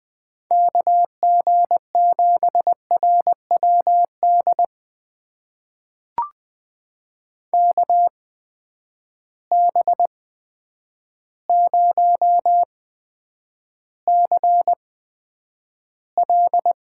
Morse code, heard as a keyed tone, about 20 wpm, spaced as for 9 wpm, 700 Hz. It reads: KG7RWD KB0CL
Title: USA Call Signs - 6 Characters - 8x Character Spacing - 20wpm